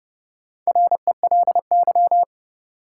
Morse code, heard as a keyed tone, about 32 wpm, 700 Hz. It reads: RELY
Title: Sentences from Top 1200 Words - New Word Drill Alt - 30wpm